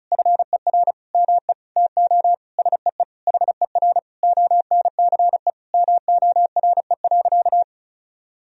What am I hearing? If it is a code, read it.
Morse code, heard as a keyed tone, about 36 wpm, 700 Hz. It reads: FER ME TO SEE HER ONCE MORE.
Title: Sentences from Top 500 Words and Abbreviations 35wpm